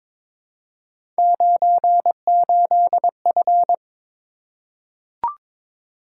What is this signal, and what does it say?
Morse code, heard as a keyed tone, about 23 wpm, 700 Hz. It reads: 98F
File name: practice / 22wpm / Three Letter-Number Combinations - 22wpm